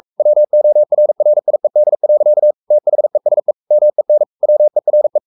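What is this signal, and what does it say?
Morse code, heard as a keyed tone, about 44 wpm, 600 Hz. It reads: WORRIED. THESE MEN WERE
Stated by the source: Morse code at 43 wpm sending Ken Follett's novel Never in English, from CW med Gnister.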